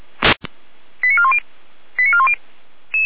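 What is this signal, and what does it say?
Five-tone selective call (ZVEI), heard as quick runs of five tones, tones of about 70 ms. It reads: 89410 89410
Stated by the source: ZVEI five-tone selective call received on an SDR, recorded by Feuerwehr Orlamünde fire brigade.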